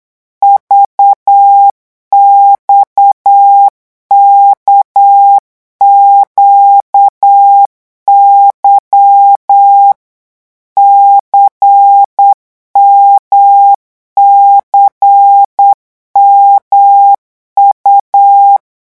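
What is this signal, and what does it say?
Morse code, heard as a keyed tone, about 8 wpm, 800 Hz. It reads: VXKQY CMCMU
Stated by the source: Morse code revision lesson at 8 words per minute, from a French amateur radio course.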